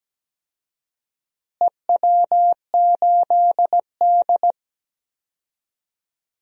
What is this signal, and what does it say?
Morse code, heard as a keyed tone, about 17 wpm, 700 Hz. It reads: EW8D
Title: DXCC-100 Call Signs - 4 Characters - Repeated 2x - 17wpm